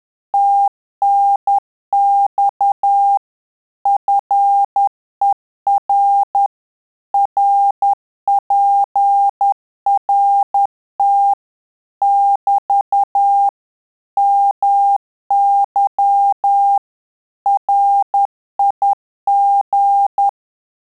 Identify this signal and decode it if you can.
Morse code, heard as a keyed tone, about 10 wpm, 800 Hz. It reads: TNX FER RPRT <BT> MY RIG
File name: Telegraphy-group / QSO CW 10 PPM